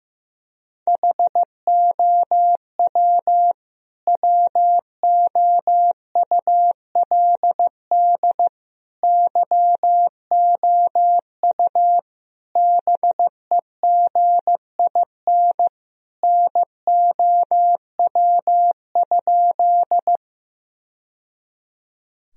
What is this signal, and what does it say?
Morse code, heard as a keyed tone, about 16 wpm, 700 Hz. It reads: HOW WOULD YOU BEGIN NOW?
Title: Sentences from Top 1000 Words - Review Part 1 - 15wpm